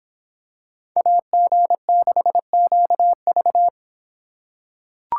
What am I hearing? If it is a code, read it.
Morse code, heard as a keyed tone, about 26 wpm, 700 Hz. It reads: AG6QV